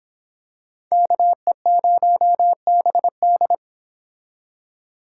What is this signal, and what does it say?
Morse code, heard as a keyed tone, about 26 wpm, 700 Hz. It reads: KE0BD